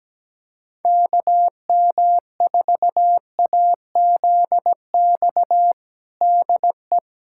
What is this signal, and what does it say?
Morse code, heard as a keyed tone, about 17 wpm, 700 Hz. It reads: KM4AZX DE
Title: USA Callsign DE Callsign - 6 Characters - 17wpm